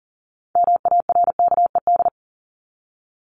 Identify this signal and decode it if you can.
Morse code, heard as a keyed tone, about 39 wpm, 700 Hz. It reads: MARKED
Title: ED Suffix - Top 100 Words with Base Words 40wpm